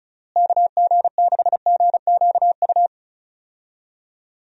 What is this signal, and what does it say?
Morse code, heard as a keyed tone, about 34 wpm, 700 Hz. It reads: KG6GQU
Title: USA Call Signs - 6 Characters - Repeated 3x - 35wpm